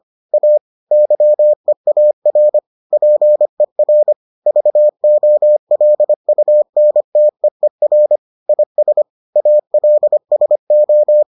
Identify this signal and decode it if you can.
Morse code, heard as a keyed tone, about 25 wpm, 600 Hz. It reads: A YEAR PER VOLUNTEER IS ALSO